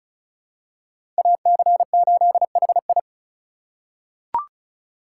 Morse code, heard as a keyed tone, about 36 wpm, 700 Hz. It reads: AC8HI
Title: USA Call Signs - 3x Character Spacing - 35wpm